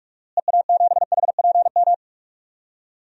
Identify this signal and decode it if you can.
Morse code, heard as a keyed tone, about 44 wpm, 700 Hz. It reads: EA7HPM
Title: DXCC-100 Call Signs - 6 Characters - Repeated 3x - 45wpm